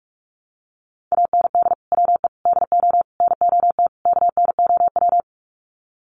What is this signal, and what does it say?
Morse code, heard as a keyed tone, about 46 wpm, 700 Hz. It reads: AND WE DO NOT KNOW
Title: Sets of 5 Words Encore 45wpm